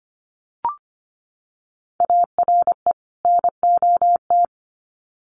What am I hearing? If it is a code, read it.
Morse code, heard as a keyed tone, about 25 wpm, 700 Hz. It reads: ARE NOT
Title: Sets of 2 Words 25wpm